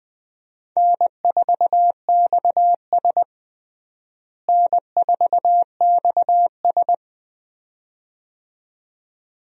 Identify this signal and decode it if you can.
Morse code, heard as a keyed tone, about 20 wpm, 700 Hz. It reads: N4XS N4XS
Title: USA Call Signs - 4 Characters - Repeated 2x - 20wpm